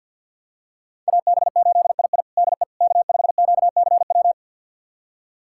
Morse code, heard as a keyed tone, about 45 wpm, 700 Hz. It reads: AB8II DE K5QCW
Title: USA Callsign DE Callsign - 5 Characters - 50wpm